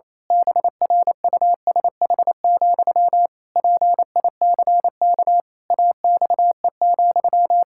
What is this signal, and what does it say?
Morse code, heard as a keyed tone, about 28 wpm, 700 Hz. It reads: BRUSH, PICK AXE,
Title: Pangram Challenge v2 - 28wpm